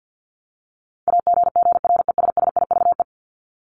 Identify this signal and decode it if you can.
Morse code, heard as a keyed tone, about 48 wpm, 700 Hz. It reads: AGGRESSIVE
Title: Top 1501-1750 Words 50wpm